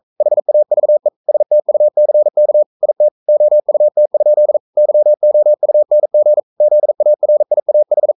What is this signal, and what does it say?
Morse code, heard as a keyed tone, about 44 wpm, 600 Hz. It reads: HAVE STUCK IT OUT? YOUNG ZARIAH
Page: Romanen Never af Ken Follet på engelsk.